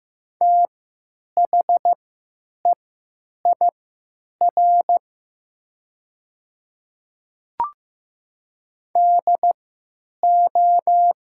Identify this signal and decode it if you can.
Morse code, heard as a keyed tone, about 15 wpm, spaced as for 9 wpm, 700 Hz. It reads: THEIR DO
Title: 3x Character Spacing - Top 100 Words - 15wpm